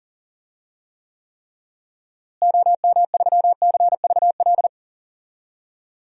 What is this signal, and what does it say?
Morse code, heard as a keyed tone, about 40 wpm, 700 Hz. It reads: OM3CVL
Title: DXCC-100 Call Signs - 6 Characters - Repeated 3x - 40wpm